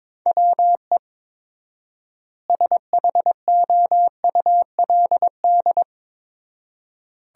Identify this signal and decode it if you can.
Morse code, heard as a keyed tone, about 22 wpm, 700 Hz. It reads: WE SHOULD
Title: 4x Word-Spacing - Sentences from Top 300 Words - 22wpm